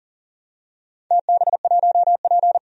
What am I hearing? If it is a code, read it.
Morse code, heard as a keyed tone, about 43 wpm, 700 Hz. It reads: TB1P
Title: DXCC-100 Call Signs - 4 Characters - 40wpm